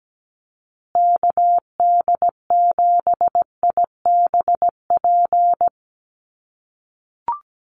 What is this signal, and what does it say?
Morse code, heard as a keyed tone, about 17 wpm, 700 Hz. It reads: KD7IBP
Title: USA Call Signs Repeated 3x 17wpm